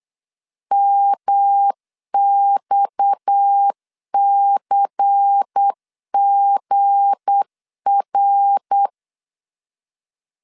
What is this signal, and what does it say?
Morse code, heard as a keyed tone, about 8 wpm, 800 Hz. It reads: MXCGR
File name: LEZIONI / 8 YJ45 / 8 YJ4 8wpm 3246